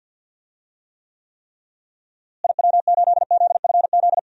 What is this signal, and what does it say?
Morse code, heard as a keyed tone, about 51 wpm, 700 Hz. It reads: IW8ZFZ
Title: DXCC-100 Call Signs - 6 Characters - Repeated 3x - 50wpm